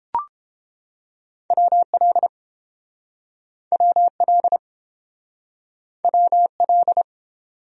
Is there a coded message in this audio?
Yes: Morse code from 1.5 to 7.0 s, heard as a keyed tone, about 30 wpm, 700 Hz. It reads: WL WL WL